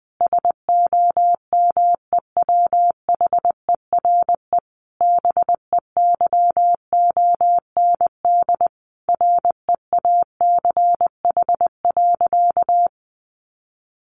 Morse code, heard as a keyed tone, about 20 wpm, 700 Hz. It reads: SOMEWHERE BEYOND REACH.
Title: Sentences from Top 1400 Words 20wpm